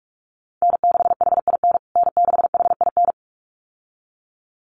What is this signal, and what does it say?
Morse code, heard as a keyed tone, about 44 wpm, 700 Hz. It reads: N6HIN N6HIN
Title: USA Call Signs Repeated 2x 45wpm